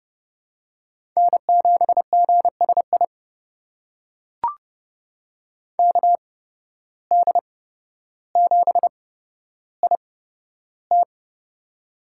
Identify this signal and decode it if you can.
Morse code, heard as a keyed tone, about 30 wpm, spaced as for 13 wpm, 700 Hz. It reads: N7GSI KD7IT
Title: USA Call Signs - 5 Characters - 8x Character Spacing - 30wpm